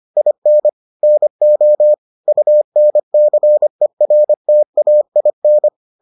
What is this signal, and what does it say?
Morse code, heard as a keyed tone, about 25 wpm, 600 Hz. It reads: IN NO UNCERTAIN